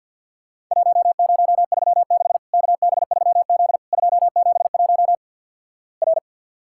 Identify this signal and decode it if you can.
Morse code, heard as a keyed tone, about 50 wpm, 700 Hz. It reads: 1036 KB3Z 271 R
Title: Worked All Europe DX Contest - QTC Exchange with Repeats - 50wpm